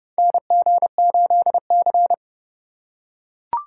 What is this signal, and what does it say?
Morse code, heard as a keyed tone, about 30 wpm, 700 Hz. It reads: NG8C
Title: USA Call Signs - 4 Characters - 3x Character Spacing - 30wpm